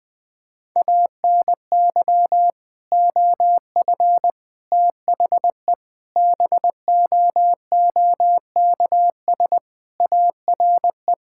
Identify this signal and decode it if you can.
Morse code, heard as a keyed tone, about 20 wpm, 700 Hz. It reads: ANY OF THE BOOKS ARE
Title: Sentences with 3-Word Phrases - Spoken Beginning - 20wpm